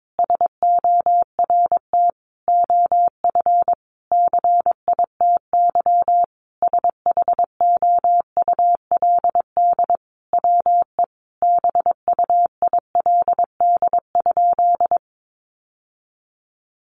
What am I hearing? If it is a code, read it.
Morse code, heard as a keyed tone, about 22 wpm, 700 Hz. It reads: SORT OF CITY SHOULD WE BUILD?
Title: Sentences from Top 1300 Words 22wpm